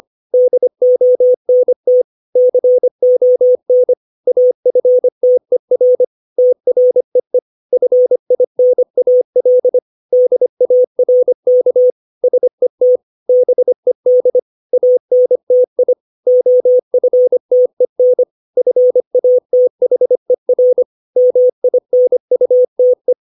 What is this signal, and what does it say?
Morse code, heard as a keyed tone, about 25 wpm, 500 Hz. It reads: DONT CON AFTER TREE FINAL DARK SET BED ANTI OFTEN FATHER MINUTE
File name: 632 words / 632 Words 25WPM